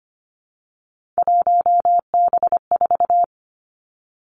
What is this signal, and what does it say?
Morse code, heard as a keyed tone, about 25 wpm, 700 Hz. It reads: 1B4
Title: Three Challenging Letter-Number Combinations 25wpm